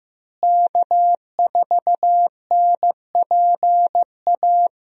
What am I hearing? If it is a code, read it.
Morse code, heard as a keyed tone, about 15 wpm, 700 Hz. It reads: K4NPA
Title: USA Call Signs - 5 Characters - Repeated 3x - 15wpm